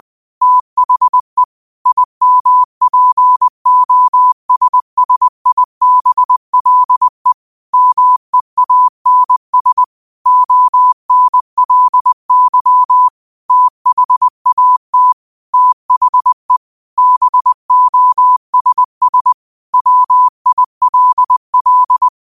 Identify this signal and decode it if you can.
Morse code, heard as a keyed tone, about 20 wpm, 1000 Hz. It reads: THE IMPOSSIBLE MEANS ONLY THAT THE BOSS WILL
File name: qotdfiles / 20wpm